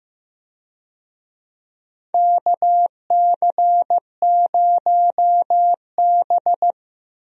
Morse code, heard as a keyed tone, about 15 wpm, 700 Hz. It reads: KC0B
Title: USA Call Signs - 4 Characters - Repeated 2x - 15wpm